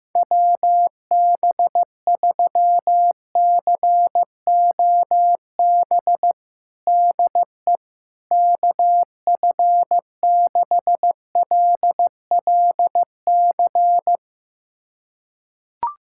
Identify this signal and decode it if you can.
Morse code, heard as a keyed tone, about 16 wpm, 700 Hz. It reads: WB3COB DE KF6LLC
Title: USA Callsign DE Callsign - 6 Characters - 15wpm